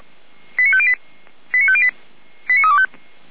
Five-tone selective call (ZVEI), heard as quick runs of five tones, tones of about 70 ms. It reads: 89508 89508 89325